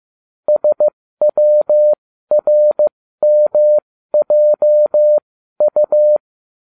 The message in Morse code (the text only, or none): SWRMJU